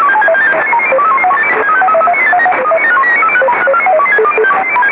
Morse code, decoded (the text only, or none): none